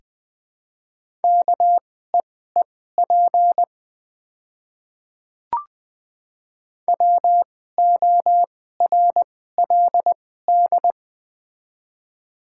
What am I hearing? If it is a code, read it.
Morse code, heard as a keyed tone, about 21 wpm, spaced as for 15 wpm, 690 Hz. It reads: KEEP WORLD
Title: 2x Character Spacing - Top 500 Words - 20wpm